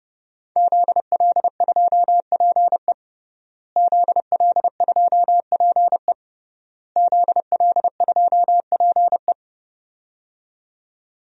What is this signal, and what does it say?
Morse code, heard as a keyed tone, about 30 wpm, 700 Hz. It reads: ZL2PE ZL2PE ZL2PE
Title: DXCC-100 Call Signs - 5 Characters - Repeated 3x - 30wpm